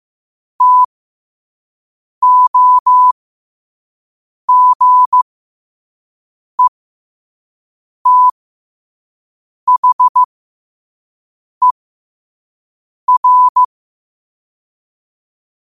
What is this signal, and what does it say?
Morse code, heard as a keyed tone, about 15 wpm, spaced as for 5 wpm, 1000 Hz. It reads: TOGETHER
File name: qotdfiles / 5wpm